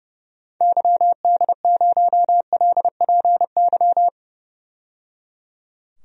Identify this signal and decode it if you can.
Morse code, heard as a keyed tone, about 30 wpm, 700 Hz. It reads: YD0LPY
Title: DXCC-100 Call Signs - 6 Characters - 30wpm